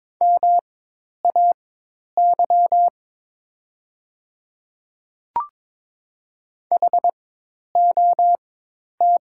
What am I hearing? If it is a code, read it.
Morse code, heard as a keyed tone, about 22 wpm, spaced as for 10 wpm, 700 Hz. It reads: MAY HOT
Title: 4x Character Spacing - Top 100 Words - 22wpm